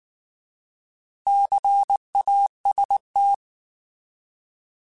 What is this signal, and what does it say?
Morse code, heard as a keyed tone, about 19 wpm, 800 Hz. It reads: CAST
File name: Alphabetical / Alphabetical mp3s / Cs @18